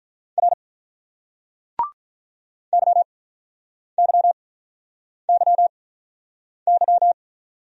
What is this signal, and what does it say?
Morse code, heard as a keyed tone, about 48 wpm, spaced as for 9 wpm, 700 Hz. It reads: R YYYY